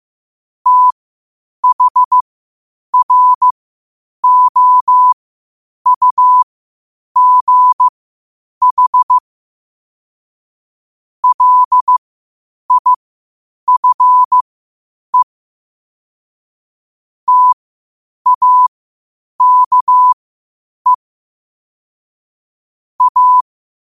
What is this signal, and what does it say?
Morse code, heard as a keyed tone, about 15 wpm, spaced as for 8 wpm, 1000 Hz. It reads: THROUGH LIFE TAKE A